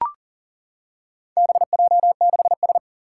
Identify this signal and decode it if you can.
Morse code, heard as a keyed tone, about 38 wpm, 700 Hz. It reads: BJ6S
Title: Four Challenging Letter-Number Combinations 40wpm